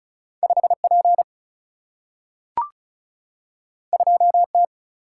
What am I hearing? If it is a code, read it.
Morse code, heard as a keyed tone, about 36 wpm, 700 Hz. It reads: 5 P 2 T
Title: Single-Letter Single-Number 35wpm